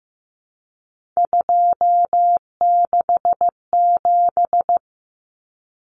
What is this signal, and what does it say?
Morse code, heard as a keyed tone, about 15 wpm, 700 Hz. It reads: 267
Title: Common Cut Numbers - 3-Digit 15wpm